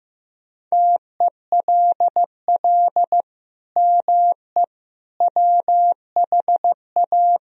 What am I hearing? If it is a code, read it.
Morse code, heard as a keyed tone, about 15 wpm, 700 Hz. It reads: TELL ME WHA
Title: Sentences from Top 1000 Words - Review Part 1 - 15wpm